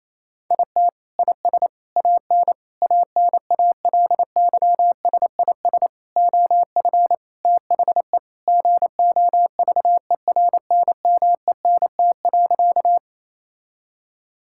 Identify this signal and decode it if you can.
Morse code, heard as a keyed tone, about 28 wpm, 700 Hz. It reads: IT IS AN ANALYSIS OF THE GOVERNMENT.
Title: Sentences from Top 1000 Words - Review Part 5 - 28wpm